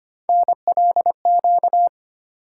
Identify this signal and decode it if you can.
Morse code, heard as a keyed tone, about 25 wpm, 700 Hz. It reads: NLQ
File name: Three Letter-Number Combinations - 25wpm